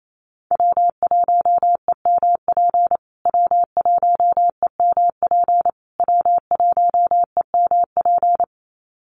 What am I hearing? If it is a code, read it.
Morse code, heard as a keyed tone, about 28 wpm, 700 Hz. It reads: W1EMP W1EMP W1EMP
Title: USA Call Signs Repeated 3x 28wpm